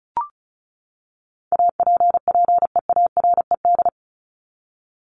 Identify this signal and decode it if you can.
Morse code, heard as a keyed tone, about 34 wpm, 700 Hz. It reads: APPEARED
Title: ED Suffix - Top 100 Words with Base Words 35wpm